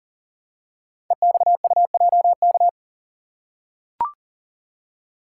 Morse code, heard as a keyed tone, about 39 wpm, 700 Hz. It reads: EXUJK
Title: Code Groups - 5 Letters - 40wpm